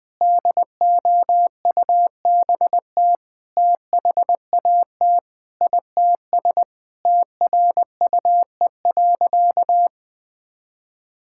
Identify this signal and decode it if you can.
Morse code, heard as a keyed tone, about 20 wpm, 700 Hz. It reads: DOUBT THAT ITS TRUE.
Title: Sentences with 7-Word Phrases - 20wpm